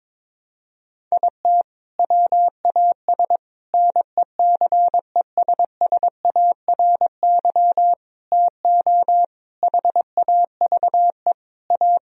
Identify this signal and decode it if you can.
Morse code, heard as a keyed tone, about 22 wpm, 700 Hz. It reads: IT WAS NECESSARY TO HAVE A
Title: Sentences from Top 1000 Words - Review Part 1 - 22wpm